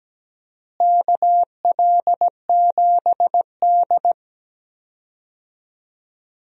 Morse code, heard as a keyed tone, about 17 wpm, 700 Hz. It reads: KL7D
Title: DXCC-100 Call Signs - 4 Characters - Repeated 3x - 17wpm